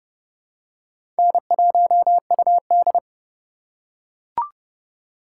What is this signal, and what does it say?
Morse code, heard as a keyed tone, about 30 wpm, 700 Hz. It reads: N1UD
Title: USA Call Signs - 4 Characters - 3x Character Spacing - 30wpm